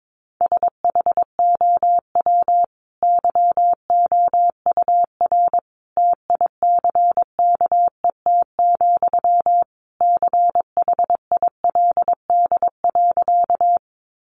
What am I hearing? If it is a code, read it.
Morse code, heard as a keyed tone, about 22 wpm, 700 Hz. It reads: SHOW YOUR TICKET, CHILD.
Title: Sentences from Top 1400 Words 22wpm